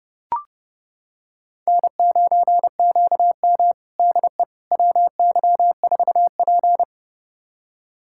Morse code, heard as a keyed tone, about 32 wpm, 700 Hz. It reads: N9QM DE WY4P
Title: USA Callsign DE Callsign - 4 Characters - 30wpm